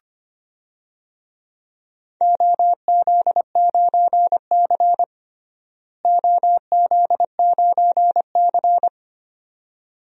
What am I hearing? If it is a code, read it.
Morse code, heard as a keyed tone, about 25 wpm, 700 Hz. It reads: OZ9C OZ9C
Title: DXCC-100 Call Signs - 4 Characters - Repeated 2x - 25wpm